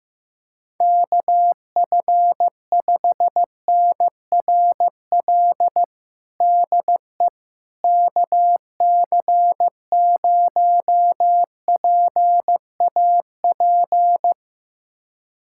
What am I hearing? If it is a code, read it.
Morse code, heard as a keyed tone, about 15 wpm, 700 Hz. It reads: KF5NRL DE KC0PAP
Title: USA Callsign DE Callsign 15wpm